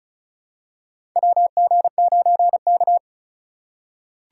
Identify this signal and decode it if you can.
Morse code, heard as a keyed tone, about 34 wpm, 700 Hz. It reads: WG9K
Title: USA Call Signs - 2x Character Spacing - 35wpm